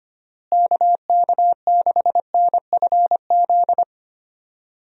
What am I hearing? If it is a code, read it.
Morse code, heard as a keyed tone, about 25 wpm, 700 Hz. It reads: KK6NFZ
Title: USA Call Signs - 6 Characters - Repeated 2x - 25wpm